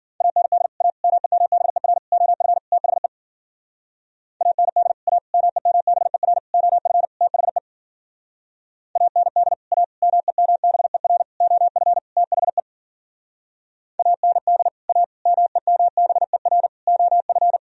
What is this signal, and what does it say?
Morse code, heard as a keyed tone, about 50 wpm, 700 Hz. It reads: AND A MEMBER OF THE AND A MEMBER OF THE AND A MEMBER OF THE AND A MEMBER OF
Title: Speed-Racing - Sets of 5 Words 40wpm